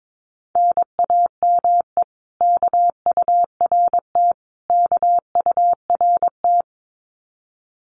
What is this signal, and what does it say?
Morse code, heard as a keyed tone, about 22 wpm, 700 Hz. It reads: NAME KURT KURT
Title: Common QSO Phrases Encore 22wpm